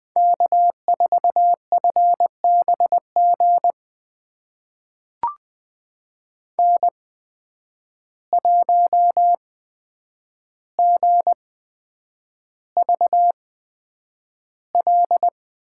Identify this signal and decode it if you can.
Morse code, heard as a keyed tone, about 20 wpm, spaced as for 9 wpm, 700 Hz. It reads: K4FBG N1GVL
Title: USA Call Signs - 5 Characters - 8x Character Spacing - 20wpm